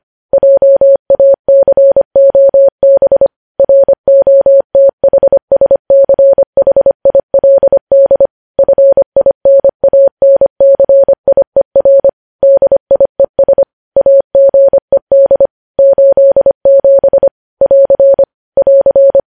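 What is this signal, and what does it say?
Morse code, heard as a keyed tone, about 25 wpm, 550 Hz. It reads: JACOB ROTHSCHILD FINANCIER DIES AGED 87 <AR> <AR>
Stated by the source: Morse code news headlines at 25 wpm.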